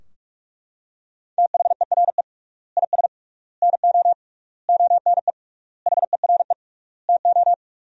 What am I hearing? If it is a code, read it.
Morse code, heard as a keyed tone, about 41 wpm, 700 Hz. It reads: THERE IS NO ONE HERE TO
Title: 3x Word-Spacing - Sentences from Top 200 Words - 45wpm